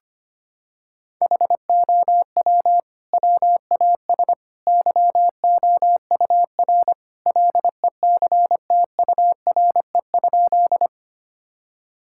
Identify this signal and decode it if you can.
Morse code, heard as a keyed tone, about 25 wpm, 700 Hz. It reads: HOW WAS YOUR LECTURE?
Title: Sentences from Top 1600 Words 25wpm